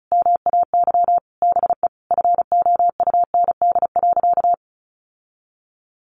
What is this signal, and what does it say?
Morse code, heard as a keyed tone, about 34 wpm, 700 Hz. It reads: MAY BE FOUND.
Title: Sentences from Top 1500 Words 35wpm